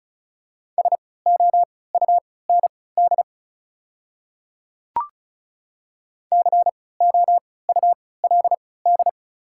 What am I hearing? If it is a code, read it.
Morse code, heard as a keyed tone, about 34 wpm, spaced as for 20 wpm, 690 Hz. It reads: SOUND COULD
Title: 3x Character Spacing - Top 100 Words - 35wpm